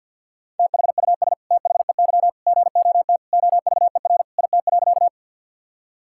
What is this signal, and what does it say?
Morse code, heard as a keyed tone, about 52 wpm, 700 Hz. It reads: THUS THEY GOT OVER IT.
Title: Sentences from Top 1000 Words - Review Part 5 - 50wpm